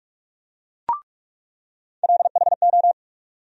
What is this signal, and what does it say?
Morse code, heard as a keyed tone, about 43 wpm, 700 Hz. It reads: LHO
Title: USA Call Sign Postfixes - 3 Letters 45wpm